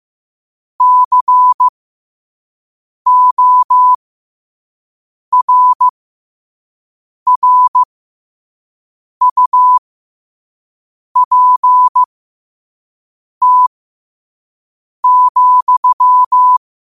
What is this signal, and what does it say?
Morse code, heard as a keyed tone, about 15 wpm, spaced as for 5 wpm, 1000 Hz. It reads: CORRUPT,
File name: qotdfiles / 5wpm